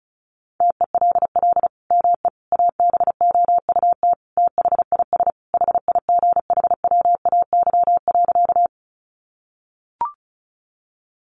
Morse code, heard as a keyed tone, about 34 wpm, 700 Hz. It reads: TELL ME ABOUT THIS HIGHWAY.
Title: Sentences from Top 1400 Words 35wpm